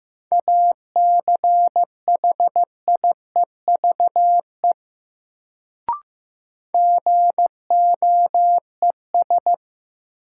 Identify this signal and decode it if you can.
Morse code, heard as a keyed tone, about 16 wpm, 700 Hz. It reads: ACHIEVE GOES
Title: Sentences from Top 1000 Words - Review Part 5 - New Word Drill - 15wpm